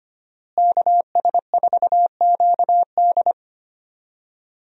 Morse code, heard as a keyed tone, about 25 wpm, 700 Hz. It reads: KS4QD
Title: USA Call Signs - 5 Characters - Repeated 2x - 25wpm